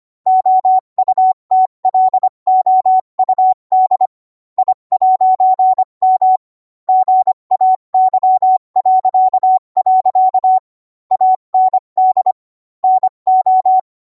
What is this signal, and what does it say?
Morse code, heard as a keyed tone, about 25 wpm, 750 Hz. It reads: OUTLOUD I'M GAY.. AND NO